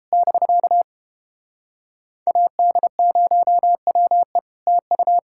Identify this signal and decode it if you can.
Morse code, heard as a keyed tone, about 29 wpm, 700 Hz. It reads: <BK> AD0WE TU